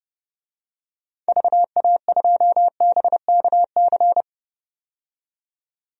Morse code, heard as a keyed tone, about 30 wpm, 700 Hz. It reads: VA2BKC
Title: DXCC-100 Call Signs - 6 Characters - 30wpm